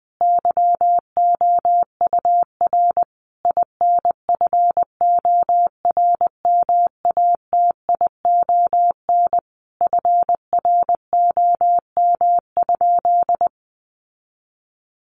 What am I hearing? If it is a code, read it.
Morse code, heard as a keyed tone, about 19 wpm, 700 Hz. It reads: YOUR INFORMATION FROM?
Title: Sentences from Top 600 Words 20wpm